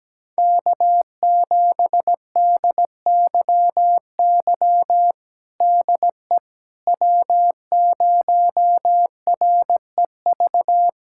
Morse code, heard as a keyed tone, about 17 wpm, 700 Hz. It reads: K7DYY DE W0REV
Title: USA Callsign DE Callsign - 5 Characters - 17wpm